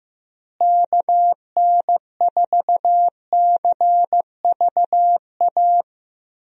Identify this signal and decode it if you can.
Morse code, heard as a keyed tone, about 15 wpm, 700 Hz. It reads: KN4CVA
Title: USA Call Signs - 6 Characters - Repeated 2x - 15wpm